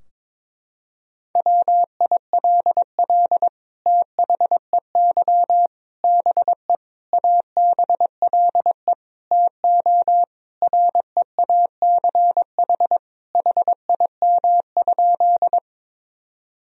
Morse code, heard as a keyed tone, about 22 wpm, 700 Hz. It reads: WILL THEY BE ABLE TO REACH HIM?
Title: Sentences from Top 1000 Words - Review Part 2 - 22wpm